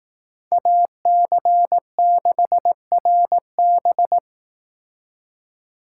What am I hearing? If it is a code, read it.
Morse code, heard as a keyed tone, about 18 wpm, 700 Hz. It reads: AC6RB
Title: Speed Racing - USA 2x2 Call Signs 15wpm